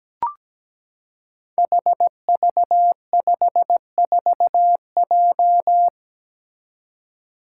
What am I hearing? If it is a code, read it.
Morse code, heard as a keyed tone, about 17 wpm, 700 Hz. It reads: HV54J
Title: Five Challenging Letter-Number Combinations 17wpm